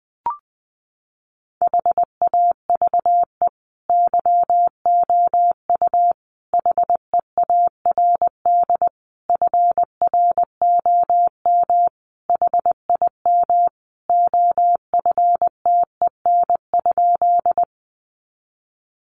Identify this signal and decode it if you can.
Morse code, heard as a keyed tone, about 19 wpm, 700 Hz. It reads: HAVE YOU HEARD FROM HIM OFTEN?
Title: Sentences from Top 400 Words 20wpm